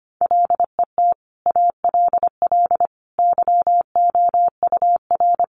Morse code, heard as a keyed tone, about 25 wpm, 700 Hz. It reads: LET ALL YOUR
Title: Sentences from Top 1500 Words 25wpm